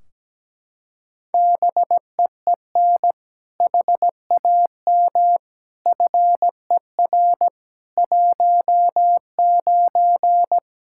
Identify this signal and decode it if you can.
Morse code, heard as a keyed tone, about 17 wpm, 700 Hz. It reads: BEEN HAM FER 19
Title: Practice QSO - Exchange 3 - More Info - 17wpm